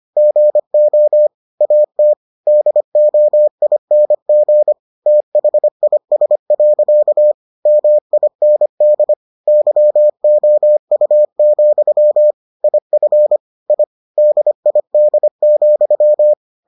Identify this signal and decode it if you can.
Morse code, heard as a keyed tone, about 25 wpm, 600 Hz. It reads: GO AT DOING THIS. MIND YOU, IF I DID,